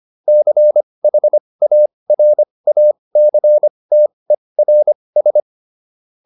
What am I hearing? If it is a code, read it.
Morse code, heard as a keyed tone, about 25 wpm, spaced as for 20 wpm, 600 Hz. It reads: CHARACTERS